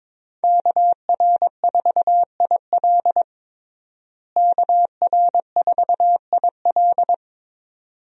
Morse code, heard as a keyed tone, about 22 wpm, 700 Hz. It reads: KR4IL KR4IL
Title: USA Call Signs - 5 Characters - Repeated 2x - 22wpm